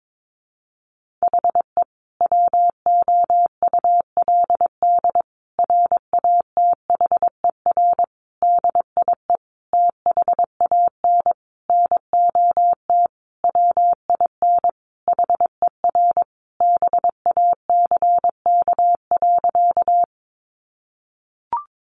Morse code, heard as a keyed tone, about 22 wpm, 700 Hz. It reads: HE WOULD RATHER DIE THAN NOT WIN HER BACK.
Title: Sentences from Top 1500 Words 22wpm